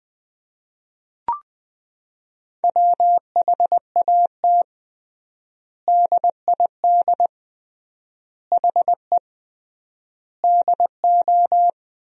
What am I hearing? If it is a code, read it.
Morse code, heard as a keyed tone, about 21 wpm, 700 Hz. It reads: WHAT DID HE DO
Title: 3x Word-Spacing - Sentences from Top 100 Words - 20wpm